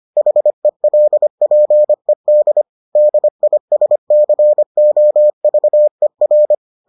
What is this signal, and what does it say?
Morse code, heard as a keyed tone, about 25 wpm, 600 Hz. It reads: HELPED DISCOVER